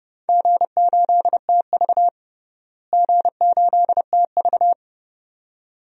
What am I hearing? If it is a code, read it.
Morse code, heard as a keyed tone, about 32 wpm, 700 Hz. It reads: G8TV G8TV
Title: DXCC-100 Call Signs - 4 Characters - Repeated 2x - 30wpm